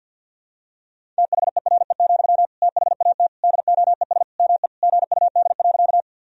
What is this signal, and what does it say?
Morse code, heard as a keyed tone, about 47 wpm, 700 Hz. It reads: THERE, THAT DOES ME GUD.
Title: Sentences from Top 200 Words and Abbreviations 50wpm